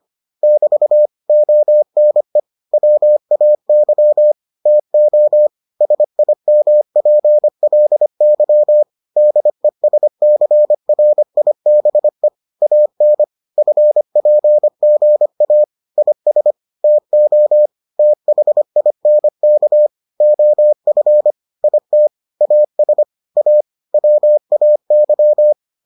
Morse code, heard as a keyed tone, about 25 wpm, 600 Hz. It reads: <BT> ONE WAY TO SIMPLY DESCRIBE AN FPGA IS TO THINK OF IT AS A WAY